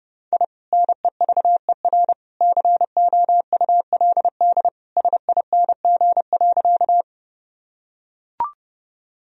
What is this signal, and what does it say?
Morse code, heard as a keyed tone, about 32 wpm, 700 Hz. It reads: I NEVER COULD SING.
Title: Sentences from Top 1000 Words - Review Part 2 - 30wpm